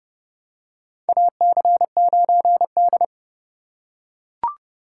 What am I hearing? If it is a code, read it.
Morse code, heard as a keyed tone, about 31 wpm, 700 Hz. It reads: AC9D
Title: USA Call Signs - 4x Character Spacing - 30wpm